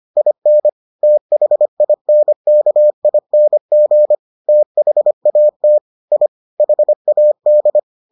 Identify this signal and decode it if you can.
Morse code, heard as a keyed tone, about 25 wpm, 600 Hz. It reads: IN THINKING THAT I HAD